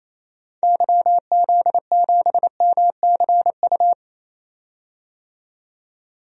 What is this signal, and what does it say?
Morse code, heard as a keyed tone, about 28 wpm, 700 Hz. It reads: YZ7MCU
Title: DXCC-100 Call Signs - 6 Characters - 28wpm